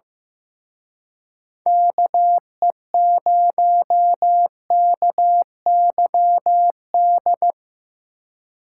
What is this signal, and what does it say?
Morse code, heard as a keyed tone, about 17 wpm, 700 Hz. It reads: KE0KYD
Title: Speed Racing - USA 2x3 Call Signs 15wpm